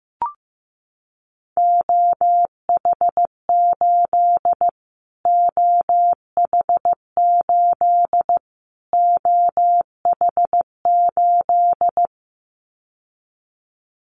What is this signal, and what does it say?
Morse code, heard as a keyed tone, about 15 wpm, 700 Hz. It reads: OH8 OH8 OH8
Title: Callsign Prefixes for Top 25 DXCC - Repeated 3x - 15wpm